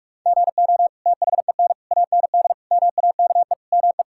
Morse code, heard as a keyed tone, about 46 wpm, 700 Hz. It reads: GO THEN AND MAKE ME
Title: Sentences from Top 1000 Words - Review Part 5 - 45wpm